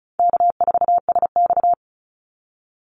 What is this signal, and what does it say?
Morse code, heard as a keyed tone, about 34 wpm, 700 Hz. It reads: K4SX
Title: Call Signs Repeated 2x - 35wpm